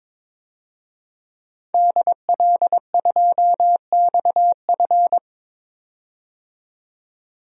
Morse code, heard as a keyed tone, about 22 wpm, 700 Hz. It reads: DL2XF
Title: DXCC-100 Call Signs - 5 Characters - 22wpm